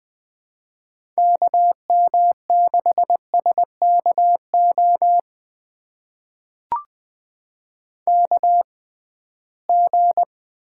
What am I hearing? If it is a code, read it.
Morse code, heard as a keyed tone, about 20 wpm, spaced as for 15 wpm, 700 Hz. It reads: KM6SKO KG
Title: USA Call Signs - 6 Characters - 6x Character Spacing - 20wpm